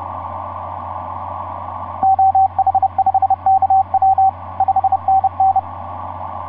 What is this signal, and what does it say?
Morse code, heard as a keyed tone, about 30 wpm, 770 Hz, with radio noise. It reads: OH5KW 5NN
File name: s79gm f6eyb etc 12m cw 24mar2010 1207